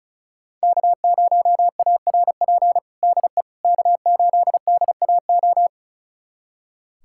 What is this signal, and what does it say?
Morse code, heard as a keyed tone, about 36 wpm, 700 Hz. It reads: K0ARP DE K8DAO
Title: USA Callsign DE Callsign - 5 Characters - 35wpm